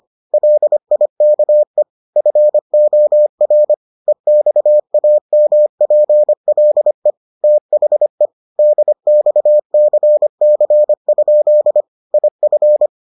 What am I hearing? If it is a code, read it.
Morse code, heard as a keyed tone, about 25 wpm, 600 Hz. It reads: LIKE FOR EXAMPLE THE DXCC? IF